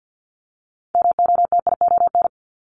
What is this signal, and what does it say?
Morse code, heard as a keyed tone, about 50 wpm, 700 Hz. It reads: MOTION
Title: Multi-letter Phonemes Repeated 3x 50wpm